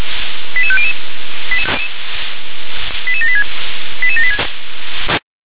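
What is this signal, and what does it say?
Five-tone selective call (ZVEI), heard as quick runs of five tones, tones of about 70 ms. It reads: 88500 88786 88786
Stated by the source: Five-tone ZVEI selective call 88761 received by SDR.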